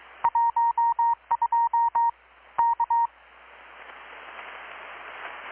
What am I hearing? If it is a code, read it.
Morse code, heard as a keyed tone, about 22 wpm, 930 Hz. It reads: 12 K
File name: blog / swl / 6-abril-2019-12695-USB-KPS